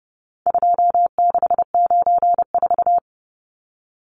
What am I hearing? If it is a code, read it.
Morse code, heard as a keyed tone, about 30 wpm, 700 Hz. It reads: 2694
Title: Numbers - 4-Digit 30wpm